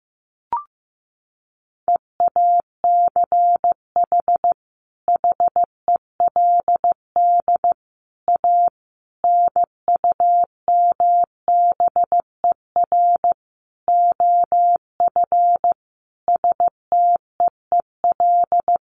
Morse code, heard as a keyed tone, about 14 wpm, 700 Hz. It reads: EACH HELD A NUMBER OF STEEL
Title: A Sound Of Thunder 15wpm